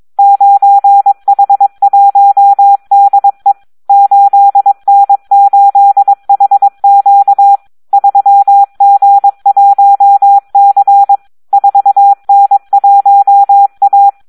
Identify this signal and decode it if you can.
Morse code, heard as a keyed tone, about 22 wpm, 800 Hz. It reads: 9H1DE 8N8HQ 3G1C 4N1A
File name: Audio practica CW / SD 1 22